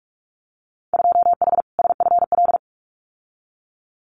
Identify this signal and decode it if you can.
Morse code, heard as a keyed tone, about 46 wpm, 700 Hz. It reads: 2H SFL
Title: Winter Field Day Exchanges Encore 45wpm